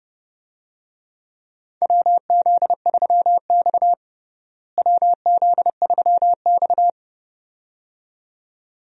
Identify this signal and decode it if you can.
Morse code, heard as a keyed tone, about 30 wpm, 700 Hz. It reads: WZ3X WZ3X
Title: USA Call Signs - 4 Characters - Repeated 2x - 30wpm